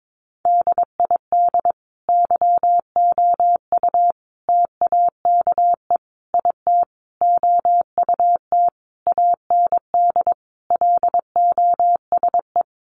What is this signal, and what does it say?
Morse code, heard as a keyed tone, about 22 wpm, 700 Hz. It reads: DID YOU TAKE IT OUT AND LOSE
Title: Sentences from Top 1100 Words 22wpm